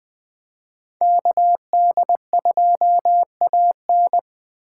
Morse code, heard as a keyed tone, about 24 wpm, 700 Hz. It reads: KD2AN
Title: Speed Racing - USA 2x2 Call Signs 20wpm